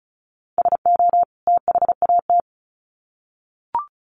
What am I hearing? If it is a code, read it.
Morse code, heard as a keyed tone, about 34 wpm, 700 Hz. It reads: SO THAT
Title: Sets of 2 Words 35wpm